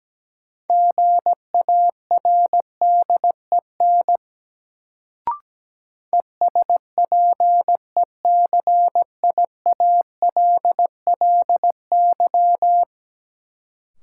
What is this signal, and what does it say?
Morse code, heard as a keyed tone, about 17 wpm, 700 Hz. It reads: GARDEN ESPECIALLY
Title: Sentences from Top 700 Words - New Word Drill - 17wpm